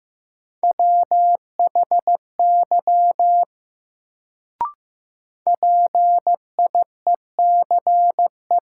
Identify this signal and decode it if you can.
Morse code, heard as a keyed tone, about 15 wpm, 700 Hz. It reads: WHY PIECE
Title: Sets of 3 Words - New Word Drill - 15wpm